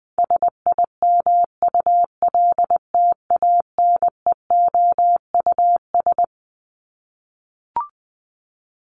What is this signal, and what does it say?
Morse code, heard as a keyed tone, about 20 wpm, 700 Hz. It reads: SIMULTANEOUS
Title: LY Suffix - Top 100 Words with Base Words 20wpm